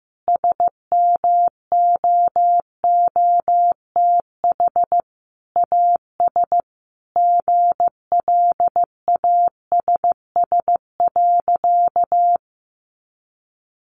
Morse code, heard as a keyed tone, about 15 wpm, 700 Hz. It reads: SMOOTH AS GLASS.
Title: Sentences from Top 1400 Words 15wpm